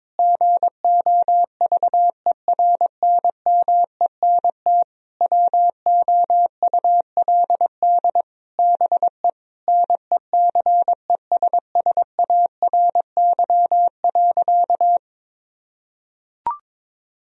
Morse code, heard as a keyed tone, about 22 wpm, 700 Hz. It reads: GOVERNMENT WOULD BE NECESSARY.
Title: Sentences from Top 1000 Words - Review Part 3 - 22wpm